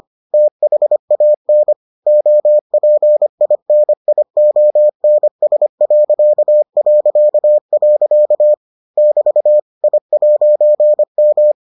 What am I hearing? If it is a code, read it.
Morse code, heard as a keyed tone, about 25 wpm, 600 Hz. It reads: THAN OPINIONS... <BT> I'M